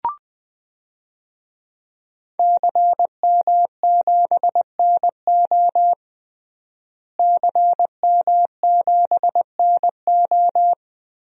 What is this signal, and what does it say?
Morse code, heard as a keyed tone, about 20 wpm, 700 Hz. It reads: CM7NO CM7NO
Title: DXCC-100 Call Signs - 5 Characters - Repeated 3x - 20wpm